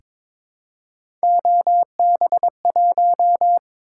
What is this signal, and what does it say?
Morse code, heard as a keyed tone, about 22 wpm, 700 Hz. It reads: OB1
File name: practice / 22wpm / Three Letter-Number Combinations - 22wpm